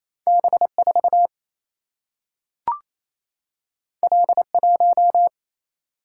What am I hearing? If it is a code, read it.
Morse code, heard as a keyed tone, about 28 wpm, 700 Hz. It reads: B4 L1
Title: Single-Letter Single-Number 28wpm